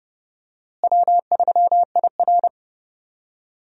Morse code, heard as a keyed tone, about 30 wpm, 700 Hz. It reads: W3IR
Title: USA Call Signs - 4 Characters - Repeated 3x - 30wpm